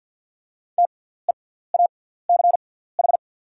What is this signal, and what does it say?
Morse code, heard as a keyed tone, about 48 wpm, spaced as for 17 wpm, 690 Hz. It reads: TEACH
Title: 6x Character Spacing - Top 500 Words - 50wpm